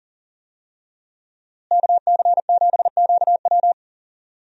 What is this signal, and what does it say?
Morse code, heard as a keyed tone, about 40 wpm, 700 Hz. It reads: KC7QW
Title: USA Call Signs - 5 Characters - Repeated 3x - 40wpm